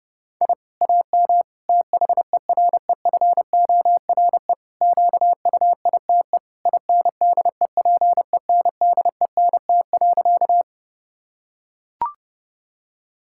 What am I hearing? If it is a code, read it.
Morse code, heard as a keyed tone, about 30 wpm, 700 Hz. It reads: I AM THEREFORE QUITE INDEPENDENT.
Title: Sentences from Top 1000 Words - Review Part 6 - 30wpm